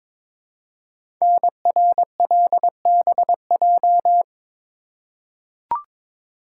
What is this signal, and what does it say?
Morse code, heard as a keyed tone, about 22 wpm, 700 Hz. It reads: NRLBJ
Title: Code Groups - 5 Letters - 22wpm